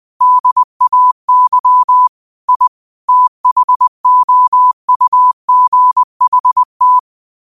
Morse code, heard as a keyed tone, about 20 wpm, 1000 Hz. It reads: DAY I THOUGHT
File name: qotdfiles / 20wpm